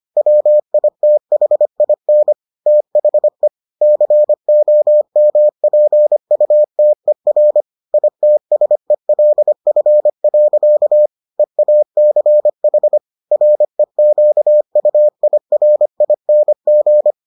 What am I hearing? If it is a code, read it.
Morse code, heard as a keyed tone, about 25 wpm, 600 Hz. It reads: WITHIN THE COMPUTER ITSELF. EACH REQUIRING